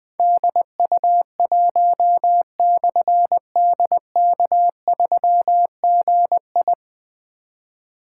Morse code, heard as a keyed tone, about 20 wpm, 700 Hz. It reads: DU1/DK3GI
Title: Difficult Call Signs 20wpm